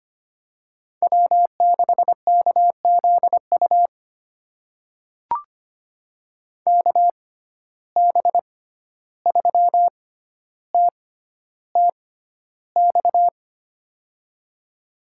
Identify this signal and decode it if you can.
Morse code, heard as a keyed tone, about 25 wpm, spaced as for 12 wpm, 700 Hz. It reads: W6KZU KB3TTX
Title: USA Call Signs - 6x Character Spacing - 25wpm